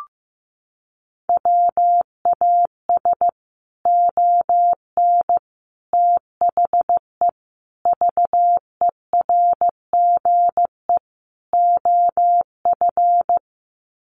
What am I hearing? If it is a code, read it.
Morse code, heard as a keyed tone, about 15 wpm, 700 Hz. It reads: WAS ON THE VERGE OF
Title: Sets of 5 Words Encore 15wpm